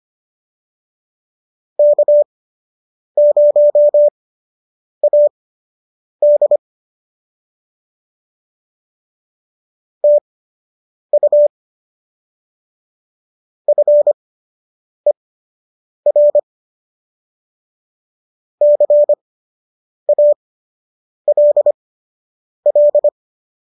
Morse code, heard as a keyed tone, about 25 wpm, spaced as for 8 wpm, 590 Hz. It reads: K0AD TU FER CALL